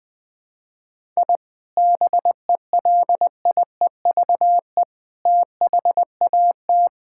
Morse code, heard as a keyed tone, about 20 wpm, 700 Hz. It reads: I BELIEVE THAT
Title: Sentences with 4-Word Phrases - Spoken Beginning - 20wpm